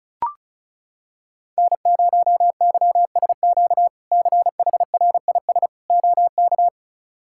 Morse code, heard as a keyed tone, about 36 wpm, 700 Hz. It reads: N0YSQ CHRIS OK
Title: K1USN SST QSO Practice - 35wpm